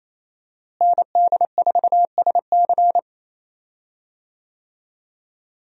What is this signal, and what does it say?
Morse code, heard as a keyed tone, about 28 wpm, 700 Hz. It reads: ND4SC
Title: USA Call Signs - 5 Characters - Repeated 2x - 28wpm